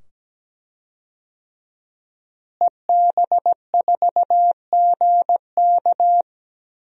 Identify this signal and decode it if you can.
Morse code, heard as a keyed tone, about 17 wpm, 700 Hz. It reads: EB4GK
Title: DXCC-100 Call Signs - 5 Characters - Repeated 2x - 17wpm